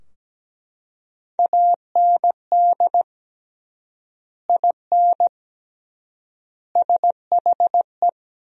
Morse code, heard as a keyed tone, about 17 wpm, 700 Hz. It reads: AND IN SHE
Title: 3x Word-Spacing - Sentences from Top 200 Words - 17wpm